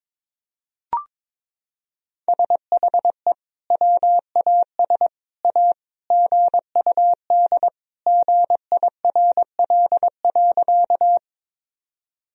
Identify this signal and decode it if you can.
Morse code, heard as a keyed tone, about 22 wpm, 700 Hz. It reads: SHE WAS A GUD GIRL.
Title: Sentences from Top 400 Words and Abbreviations 22wpm